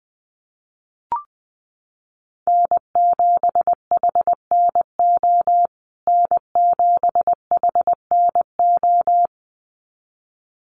Morse code, heard as a keyed tone, about 20 wpm, 700 Hz. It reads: N7HNO N7HNO
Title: USA Call Signs Repeated 2x 20wpm